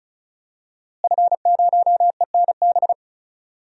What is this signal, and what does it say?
Morse code, heard as a keyed tone, about 34 wpm, 700 Hz. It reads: F0ENB
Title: DXCC-100 Call Signs - 5 Characters - Repeated 3x - 35wpm